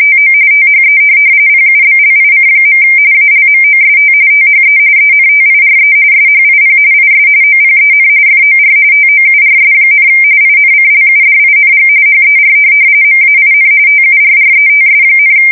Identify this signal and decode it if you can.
RTTY, about 45.45 baud. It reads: TO THE PUBLIC. ALMOST THE ENTIRE SHIP IS OPEN FOR TOURS, INCLUDING HER MAIN TURRETS,
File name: fsk170 / USS-New-Jersey